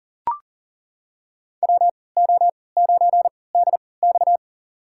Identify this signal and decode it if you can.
Morse code, heard as a keyed tone, about 40 wpm, spaced as for 23 wpm, 700 Hz. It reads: WO9DX
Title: USA Call Signs - 3x Character Spacing - 40wpm